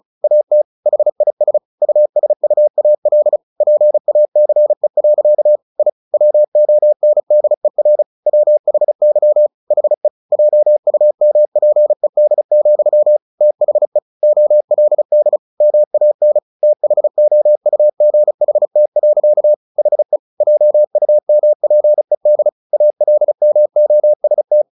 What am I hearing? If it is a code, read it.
Morse code, heard as a keyed tone, about 34 wpm, 600 Hz. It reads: AT HIS USUAL PACE. I WONDER WHY HE JUMPED, THE OLD MAN THOUGHT. HE JUMPED ALMOST
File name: the old man and the sea / the old man and the sea 1 35wpm